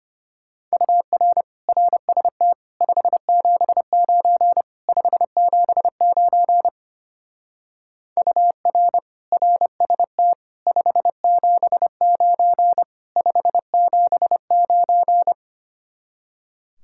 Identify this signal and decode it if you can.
Morse code, heard as a keyed tone, about 32 wpm, 700 Hz. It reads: UR RST 579 579 UR RST 579 579